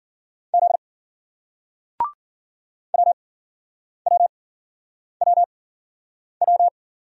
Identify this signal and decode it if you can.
Morse code, heard as a keyed tone, about 51 wpm, spaced as for 9 wpm, 700 Hz. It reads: Z WWWW